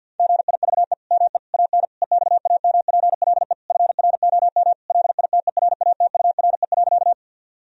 Morse code, heard as a keyed tone, about 50 wpm, 700 Hz. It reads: GIVE ME AN EXAMPLE FROM LITERATURE.
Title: Sentences from Top 1000 Words - Review Part 3 - 50wpm